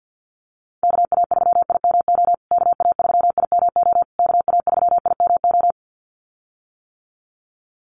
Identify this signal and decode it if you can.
Morse code, heard as a keyed tone, about 50 wpm, 700 Hz. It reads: KA3IMO KA3IMO KA3IMO
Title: USA Call Signs Repeated 3x 50wpm